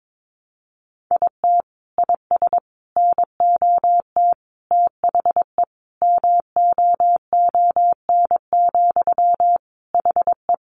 Morse code, heard as a keyed tone, about 22 wpm, 700 Hz. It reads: IT IS NOT THE MOON, HE
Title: Sentences from Top 1100 Words 22wpm